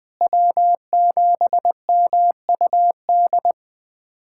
Morse code, heard as a keyed tone, about 20 wpm, 700 Hz. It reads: W7MUD
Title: USA Callsign DE Callsign - 5 Characters - 20wpm